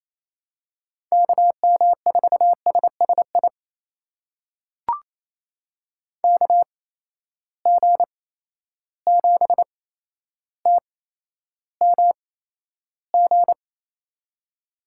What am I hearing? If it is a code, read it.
Morse code, heard as a keyed tone, about 28 wpm, spaced as for 12 wpm, 700 Hz. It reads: KM4SSI KG7TMG
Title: USA Call Signs - 6 Characters - 8x Character Spacing - 28wpm